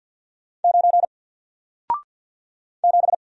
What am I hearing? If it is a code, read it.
Morse code, heard as a keyed tone, about 50 wpm, 700 Hz. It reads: 9 7
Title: Single Letter-Number Alt 50wpm